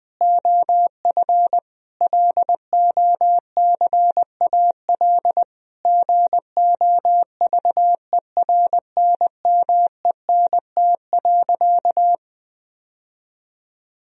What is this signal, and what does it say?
Morse code, heard as a keyed tone, about 20 wpm, 700 Hz. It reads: OF LOCAL GOVERNMENT.
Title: Sentences from Top 1000 Words - Review Part 3 - 20wpm